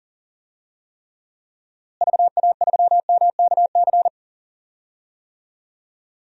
Morse code, heard as a keyed tone, about 40 wpm, 700 Hz. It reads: VA3MKC
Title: DXCC-100 Call Signs - 6 Characters - Repeated 3x - 40wpm